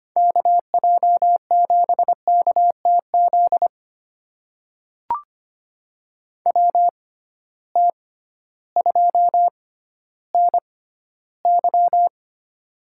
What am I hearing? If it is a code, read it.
Morse code, heard as a keyed tone, about 25 wpm, spaced as for 13 wpm, 700 Hz. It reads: KJ7KTZ WT2NY
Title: USA Call Signs - 6x Character Spacing - 25wpm